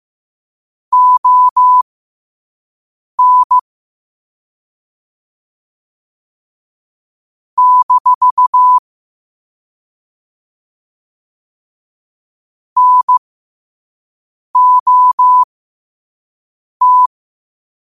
Morse code, heard as a keyed tone, about 15 wpm, spaced as for 5 wpm, 1000 Hz. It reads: ON - NOT
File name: qotdfiles / 5wpm